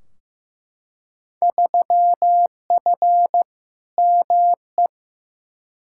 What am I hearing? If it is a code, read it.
Morse code, heard as a keyed tone, about 16 wpm, 700 Hz. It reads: 3F ME
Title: Field Day Exchanges - 15wpm